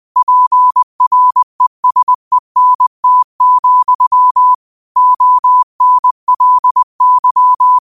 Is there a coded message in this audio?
Yes: Morse code from 0.2 to 7.8 s, heard as a keyed tone, about 20 wpm, 1000 Hz. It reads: PRESENT, ONLY